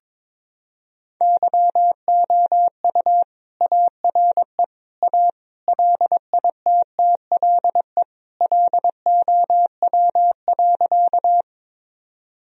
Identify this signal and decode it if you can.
Morse code, heard as a keyed tone, about 22 wpm, 700 Hz. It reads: YOU ARE A LITTLE LOW.
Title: Sentences from Top 1000 Words - Review Part 1 - 22wpm